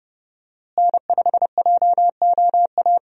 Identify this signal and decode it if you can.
Morse code, heard as a keyed tone, about 30 wpm, 700 Hz. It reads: N5JOA
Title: USA Call Signs - 5 Characters - Repeated 3x - 30wpm